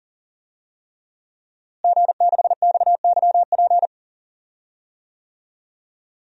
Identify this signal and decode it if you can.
Morse code, heard as a keyed tone, about 37 wpm, 700 Hz. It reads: G6XYP
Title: DXCC-100 Call Signs - 5 Characters - Repeated 3x - 40wpm